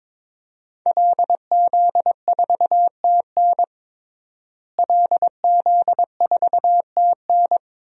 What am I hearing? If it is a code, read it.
Morse code, heard as a keyed tone, about 22 wpm, 700 Hz. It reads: LZ4TN LZ4TN
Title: DXCC-100 Call Signs - 5 Characters - Repeated 3x - 22wpm